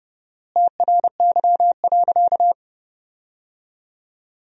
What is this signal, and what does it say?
Morse code, heard as a keyed tone, about 30 wpm, 700 Hz. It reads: TRY.
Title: 4x Word-Spacing - Sentences from Top 200 Words - 30wpm